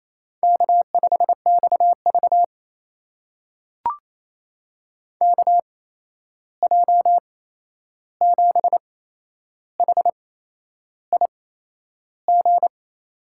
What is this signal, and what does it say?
Morse code, heard as a keyed tone, about 27 wpm, spaced as for 10 wpm, 700 Hz. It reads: K5XV KJ7HIG
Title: USA Call Signs - 8x Character Spacing - 28wpm